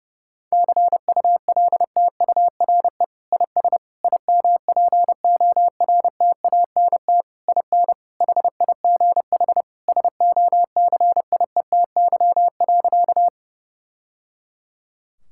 Morse code, heard as a keyed tone, about 30 wpm, 700 Hz. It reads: CULTURE IS IMPORTANT IN HIGH SOCIETY.
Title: Sentences from Top 1000 Words - Review Part 6 - 30wpm